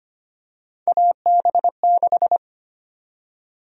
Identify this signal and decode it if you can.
Morse code, heard as a keyed tone, about 25 wpm, 700 Hz. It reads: AB6
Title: USA Call Sign Prefixes - 25wpm